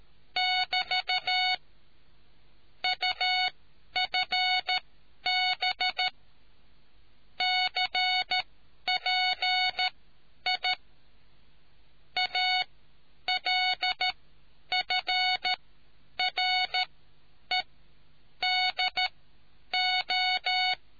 Morse code, heard as a keyed tone, about 13 wpm, spaced as for 9 wpm, 740 Hz. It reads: <BT> UFB CPI ALFREDO